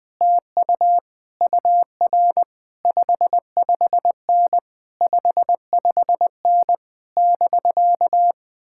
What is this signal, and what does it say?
Morse code, heard as a keyed tone, about 20 wpm, 700 Hz. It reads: TU UR 55N 55N <BK>
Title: Speed-Racing Common QSO Phrases - 20wpm